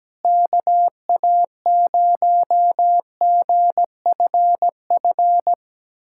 Morse code, heard as a keyed tone, about 17 wpm, 700 Hz. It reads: KA0GFF
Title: USA Callsign DE Callsign 17wpm